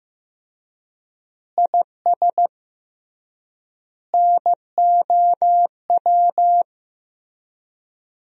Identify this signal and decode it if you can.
Morse code, heard as a keyed tone, about 15 wpm, 700 Hz. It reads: IS NOW
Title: 3x Word-Spacing - Sentences from Top 400 Words - 15wpm